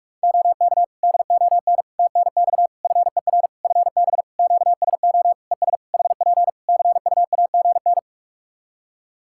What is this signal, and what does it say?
Morse code, heard as a keyed tone, about 46 wpm, 700 Hz. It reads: OK DON TNX FER FB QSO ES HP CUAGN
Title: Practice QSO - Exchange 4 - Ending - 45wpm